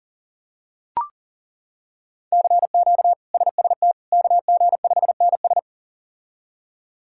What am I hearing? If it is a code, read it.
Morse code, heard as a keyed tone, about 40 wpm, 700 Hz. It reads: CQ SST KG5NS
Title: K1USN SST QSO Practice - 40wpm